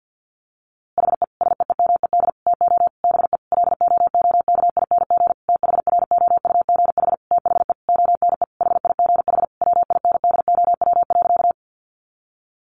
Morse code, heard as a keyed tone, about 50 wpm, 700 Hz. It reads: HE SEEMED TO BE LOOKING THROUGH THE ONE HIGH WINDOW.
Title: A Sound Of Thunder 50wpm